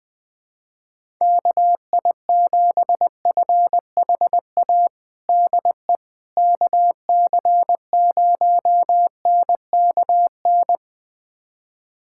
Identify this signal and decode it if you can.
Morse code, heard as a keyed tone, about 20 wpm, 700 Hz. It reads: KI7FHA DE KC0NKN
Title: USA Callsign DE Callsign 20wpm